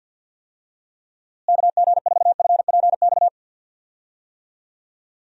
Morse code, heard as a keyed tone, about 51 wpm, 700 Hz. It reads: KG4FPX
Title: USA Call Signs - 6 Characters - Repeated 3x - 50wpm